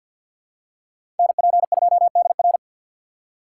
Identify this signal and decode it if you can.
Morse code, heard as a keyed tone, about 50 wpm, 700 Hz. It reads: NP2DR
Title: USA Call Signs - 5 Characters - 3x Character Spacing - 50wpm